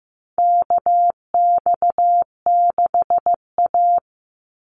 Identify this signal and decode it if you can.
Morse code, heard as a keyed tone, about 15 wpm, 700 Hz. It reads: KX6A
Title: NA SOTA Chaser Callsigns 15wpm